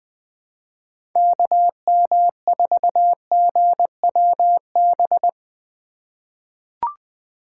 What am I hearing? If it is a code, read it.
Morse code, heard as a keyed tone, about 20 wpm, 700 Hz. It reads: KM4GWB
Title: USA Call Signs - 6 Characters - 2x Character Spacing - 20wpm